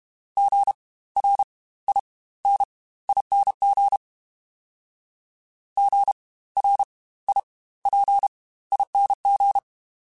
Morse code, heard as a keyed tone, about 32 wpm, spaced as for 18 wpm, 800 Hz. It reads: GRINING GRIPING
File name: Spaced Alphas / Spaced Alphas mp3s / Gs sp @30